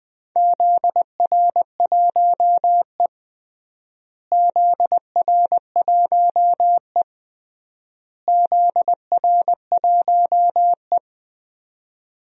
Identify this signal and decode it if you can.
Morse code, heard as a keyed tone, about 20 wpm, 700 Hz. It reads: ZR1E ZR1E ZR1E
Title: DXCC-100 Call Signs - 4 Characters - Repeated 3x - 20wpm